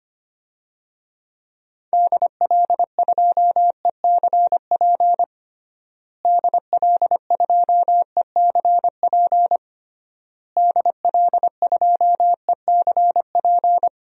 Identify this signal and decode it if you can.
Morse code, heard as a keyed tone, about 25 wpm, 700 Hz. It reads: DL2ECP DL2ECP DL2ECP
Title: DXCC-100 Call Signs - 6 Characters - Repeated 3x - 25wpm